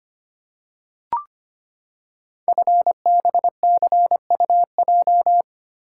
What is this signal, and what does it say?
Morse code, heard as a keyed tone, about 25 wpm, 700 Hz. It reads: FBCUJ
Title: Code Groups - 5 Letters - 25wpm